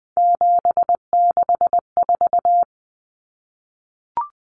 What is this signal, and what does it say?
Morse code, heard as a keyed tone, about 20 wpm, 700 Hz. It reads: 764
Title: Numbers - 3-Digit 20wpm